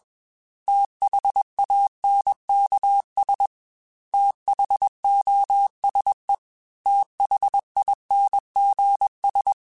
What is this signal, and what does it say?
Morse code, heard as a keyed tone, about 21 wpm, 800 Hz. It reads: THANKS THOSE THINGS
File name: TH Runs / TH Runs mp3s / TH Runs @20